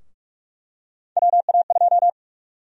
Morse code, heard as a keyed tone, about 45 wpm, 700 Hz. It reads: WA2
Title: USA Call Sign Prefixes - Repeated 2x - 45wpm